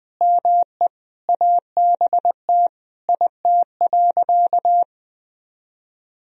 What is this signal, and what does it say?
Morse code, heard as a keyed tone, about 20 wpm, 700 Hz. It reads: ME ABT IT.
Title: Sentences from Top 200 Words and Abbreviations 20wpm